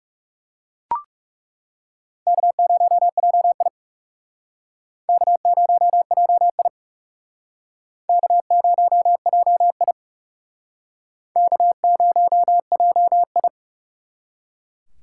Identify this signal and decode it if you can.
Morse code, heard as a keyed tone, about 39 wpm, 700 Hz. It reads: K0JI K0JI K0JI K0JI